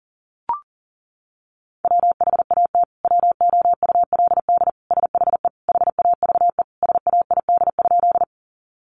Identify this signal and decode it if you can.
Morse code, heard as a keyed tone, about 40 wpm, 700 Hz. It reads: WHAT WOULD SHE HAVE SAID?
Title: Sentences from Top 100 Words 40wpm